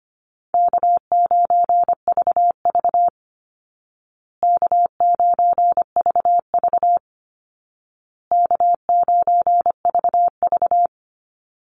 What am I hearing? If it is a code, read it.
Morse code, heard as a keyed tone, about 25 wpm, 700 Hz. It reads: K9VV K9VV K9VV
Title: Call Signs Repeated 2x - 25wpm